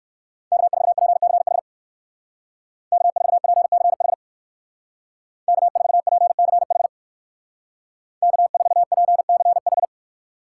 Morse code, heard as a keyed tone, about 62 wpm, 700 Hz. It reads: K4PCH K4PCH K4PCH K4PCH